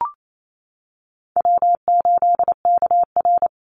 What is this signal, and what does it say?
Morse code, heard as a keyed tone, about 28 wpm, 700 Hz. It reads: W8KR
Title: Easy Call Signs - 28wpm